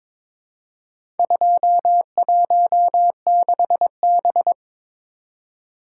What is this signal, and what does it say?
Morse code, heard as a keyed tone, about 22 wpm, 700 Hz. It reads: 216B
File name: practice / 22wpm / Four Letter-Number Combinations - 22wpm